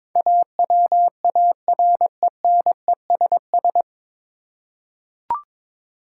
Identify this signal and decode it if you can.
Morse code, heard as a keyed tone, about 22 wpm, 700 Hz. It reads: AWARENESS
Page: Morse Code Ninja practice 22wpm